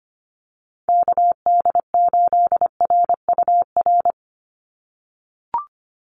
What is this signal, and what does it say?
Morse code, heard as a keyed tone, about 25 wpm, 700 Hz. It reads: KD8RUR
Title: USA Call Signs 25wpm